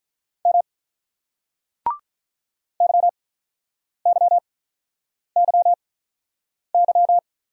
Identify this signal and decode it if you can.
Morse code, heard as a keyed tone, about 48 wpm, spaced as for 9 wpm, 690 Hz. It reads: M YYYY